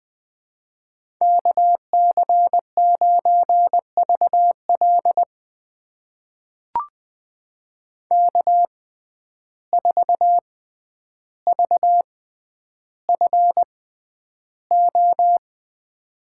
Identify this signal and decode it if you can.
Morse code, heard as a keyed tone, about 20 wpm, spaced as for 10 wpm, 700 Hz. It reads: KC9VL K4VFO
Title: USA Call Signs - 5 Characters - 6x Character Spacing - 20wpm